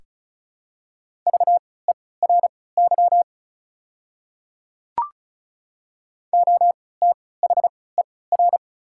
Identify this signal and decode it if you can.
Morse code, heard as a keyed tone, about 34 wpm, spaced as for 20 wpm, 700 Hz. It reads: VERY OTHER
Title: 3x Character Spacing - Top 200 Words - 35wpm